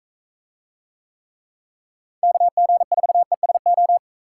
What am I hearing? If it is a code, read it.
Morse code, heard as a keyed tone, about 44 wpm, 700 Hz. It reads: KG4ESO